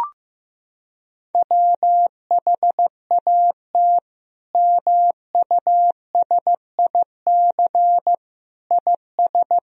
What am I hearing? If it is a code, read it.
Morse code, heard as a keyed tone, about 15 wpm, 700 Hz. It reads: WHAT MUSIC IS
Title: Sentences from Top 1000 Words - Review Part 1 - 15wpm